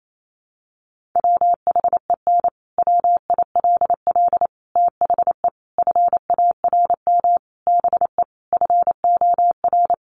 Morse code, heard as a keyed tone, about 28 wpm, 700 Hz. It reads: WHEN WILL THE FARM BE FOR
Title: Sentences from Top 1400 Words 28wpm